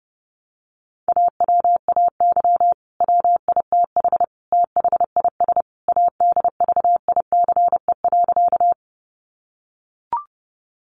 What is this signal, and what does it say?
Morse code, heard as a keyed tone, about 30 wpm, 700 Hz. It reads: AWAY WITH THIS ADVICE.
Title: Sentences from Top 1100 Words 30wpm